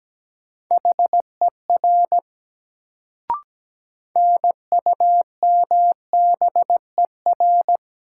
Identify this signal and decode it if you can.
Morse code, heard as a keyed tone, about 17 wpm, 700 Hz. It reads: HER NUMBER
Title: Sets of 2 Words - New Word Drill - 17wpm